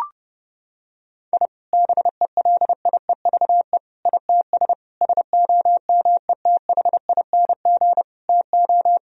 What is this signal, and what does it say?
Morse code, heard as a keyed tone, about 30 wpm, 700 Hz. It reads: I BELIEVE ITS SOMETHING TO
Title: Sentences from Top 1000 Words - Review Part 3 - 30wpm